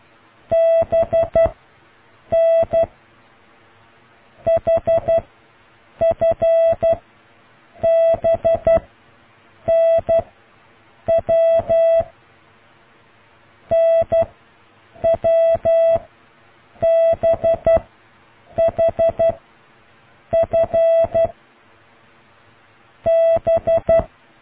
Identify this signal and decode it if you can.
Morse code, heard as a keyed tone, about 12 wpm, 660 Hz. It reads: B N H F B N W N W B H F B